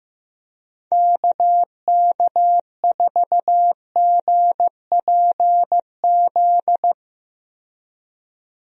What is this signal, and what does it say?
Morse code, heard as a keyed tone, about 15 wpm, 700 Hz. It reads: KK4GPZ
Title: USA Call Signs - 6 Characters - 15wpm